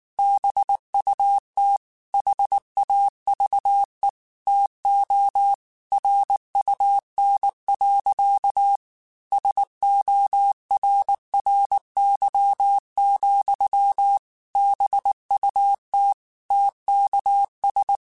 Morse code, heard as a keyed tone, about 19 wpm, 800 Hz. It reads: BUT HAVE TO RUN. SORRY, BUT TKS
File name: QSOs / QSOs-mp3s / kc7sac@18